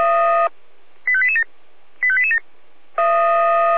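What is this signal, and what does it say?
Five-tone selective call (ZVEI), heard as quick runs of five tones, tones of about 70 ms. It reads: 85887 85887
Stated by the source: ZVEI five-tone selective call received by SDR.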